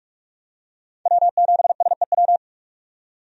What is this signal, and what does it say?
Morse code, heard as a keyed tone, about 46 wpm, 700 Hz. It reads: W7SEW
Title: USA Call Signs - 5 Characters - 45wpm